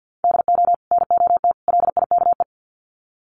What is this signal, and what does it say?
Morse code, heard as a keyed tone, about 50 wpm, 700 Hz. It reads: DO NOT LIKE
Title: Sets of 3 Words 50wpm